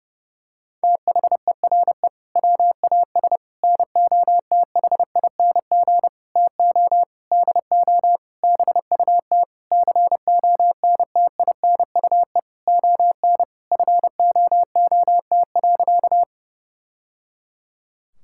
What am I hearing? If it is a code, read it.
Morse code, heard as a keyed tone, about 32 wpm, 700 Hz. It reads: THERE WAS NOTHING TO DO BUT CONTINUE ON FOOT.
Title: Sentences from Top 1000 Words - Review Part 6 - 30wpm